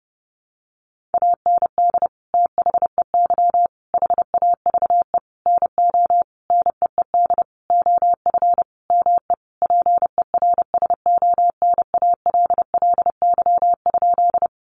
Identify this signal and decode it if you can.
Morse code, heard as a keyed tone, about 28 wpm, 700 Hz. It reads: AND THEY HAVE NO NEED OF ME PERSONALLY?
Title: Sentences from Top 1400 Words 30wpm